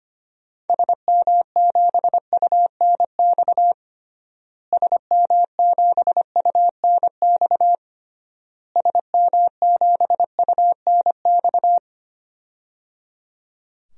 Morse code, heard as a keyed tone, about 25 wpm, 700 Hz. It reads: SM7UNX SM7UNX SM7UNX
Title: DXCC-100 Call Signs - 6 Characters - Repeated 3x - 25wpm